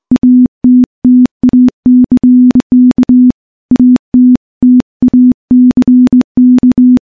Morse code, heard as a keyed tone, about 21 wpm, 250 Hz. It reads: ATTACK ATTACK
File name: morse / morse-training-20WPM